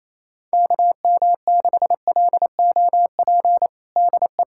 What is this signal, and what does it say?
Morse code, heard as a keyed tone, about 28 wpm, 700 Hz. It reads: KM6LOP DE
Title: USA Callsign DE Callsign - 6 Characters - 28wpm